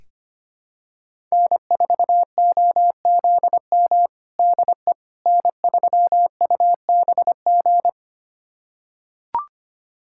Morse code, heard as a keyed tone, about 25 wpm, 700 Hz. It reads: N4OZM DE N3UBG
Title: USA Callsign DE Callsign - 5 Characters - 25wpm